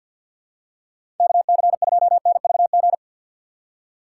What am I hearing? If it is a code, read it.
Morse code, heard as a keyed tone, about 52 wpm, 700 Hz. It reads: KC2NVG
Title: USA Call Signs - 6 Characters - 4x Character Spacing - 50wpm